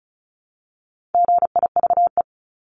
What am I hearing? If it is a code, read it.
Morse code, heard as a keyed tone, about 36 wpm, 700 Hz. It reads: GIVE
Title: Multi-letter Phonemes Repeated 3x 35wpm